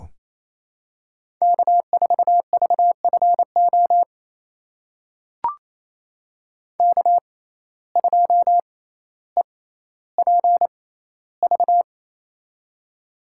Morse code, heard as a keyed tone, about 27 wpm, spaced as for 14 wpm, 700 Hz. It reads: K4VFO K2EPV
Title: USA Call Signs - 5 Characters - 6x Character Spacing - 28wpm